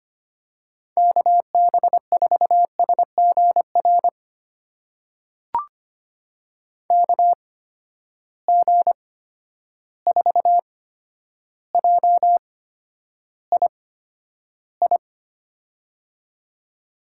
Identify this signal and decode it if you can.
Morse code, heard as a keyed tone, about 25 wpm, spaced as for 11 wpm, 700 Hz. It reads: KB4SGR KG4JII
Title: USA Call Signs - 6 Characters - 8x Character Spacing - 25wpm